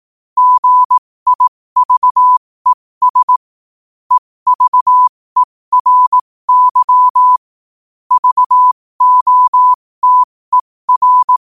Morse code, heard as a keyed tone, about 18 wpm, spaced as for 15 wpm, 1000 Hz. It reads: GIVES EVERY VOTER
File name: 15wpm